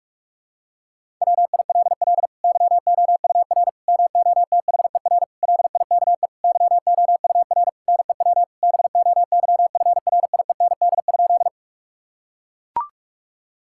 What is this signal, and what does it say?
Morse code, heard as a keyed tone, about 44 wpm, 700 Hz. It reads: WILL YOUR MOTHER LIKE YOUR NEW BOYFRIEND?
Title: Sentences from Top 1000 Words - Review Part 5 - 45wpm